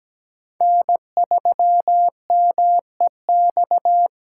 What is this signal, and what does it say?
Morse code, heard as a keyed tone, about 17 wpm, 700 Hz. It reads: N3MEX
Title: USA Call Signs - 3x Character Spacing - 17wpm